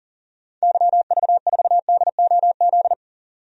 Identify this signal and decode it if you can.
Morse code, heard as a keyed tone, about 40 wpm, 700 Hz. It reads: YV4DOZ
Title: DXCC-100 Call Signs - 6 Characters - 40wpm